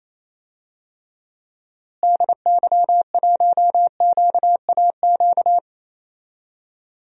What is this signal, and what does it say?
Morse code, heard as a keyed tone, about 28 wpm, 700 Hz. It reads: DY1QAQ
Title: DXCC-100 Call Signs - 6 Characters - Repeated 2x - 28wpm